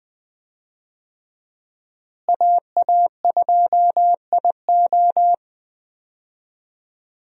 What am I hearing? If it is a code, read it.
Morse code, heard as a keyed tone, about 21 wpm, 700 Hz. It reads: AA2IO
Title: USA Call Signs - 5 Characters - Repeated 3x - 20wpm